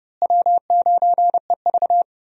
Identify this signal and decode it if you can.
Morse code, heard as a keyed tone, about 32 wpm, 700 Hz. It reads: W9EV
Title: USA Call Signs - 4 Characters - Repeated 2x - 30wpm